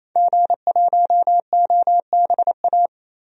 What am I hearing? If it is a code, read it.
Morse code, heard as a keyed tone, about 28 wpm, 700 Hz. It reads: G1OBA
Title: DXCC-100 Call Signs - 5 Characters - 28wpm